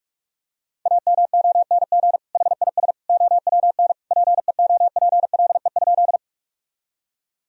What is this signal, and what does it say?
Morse code, heard as a keyed tone, about 46 wpm, 700 Hz. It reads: AMONG HIS OWN PEOPLE?
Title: Sentences from Top 1000 Words - Review Part 3 - 45wpm